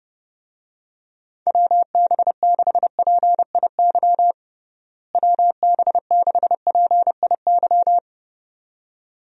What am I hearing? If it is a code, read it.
Morse code, heard as a keyed tone, about 30 wpm, 700 Hz. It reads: WB6PIY WB6PIY
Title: USA Call Signs - 6 Characters - Repeated 2x - 30wpm